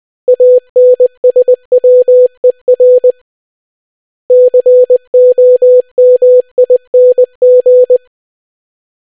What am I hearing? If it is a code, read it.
Morse code, heard as a keyed tone, about 20 wpm, 500 Hz. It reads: ANSWER COMING